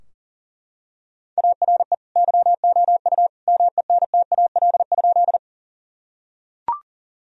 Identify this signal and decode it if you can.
Morse code, heard as a keyed tone, about 37 wpm, 700 Hz. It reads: ARE YOU MENTAL?
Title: Sentences from Top 1000 Words - Review Part 4 - 40wpm